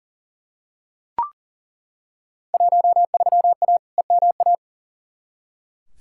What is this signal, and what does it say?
Morse code, heard as a keyed tone, about 37 wpm, 700 Hz. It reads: 13A EMA
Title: Field Day Exchanges Encore - 40wpm